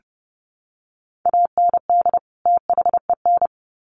Morse code, heard as a keyed tone, about 30 wpm, 700 Hz. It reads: AND THEN
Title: Sets of 2 Words 30wpm